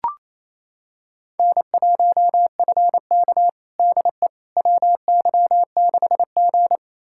Morse code, heard as a keyed tone, about 28 wpm, 700 Hz. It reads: N1FK DE WY6G
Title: USA Callsign DE Callsign - 4 Characters - 28wpm